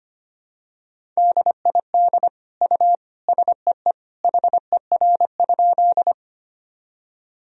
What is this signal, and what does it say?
Morse code, heard as a keyed tone, about 25 wpm, 700 Hz. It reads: DID U SEE HER?
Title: Sentences from Top 100 Words and Abbreviations 25wpm